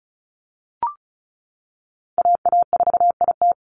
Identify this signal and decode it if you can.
Morse code, heard as a keyed tone, about 34 wpm, 700 Hz. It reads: AA4IT
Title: NA SOTA Chaser Callsigns 35wpm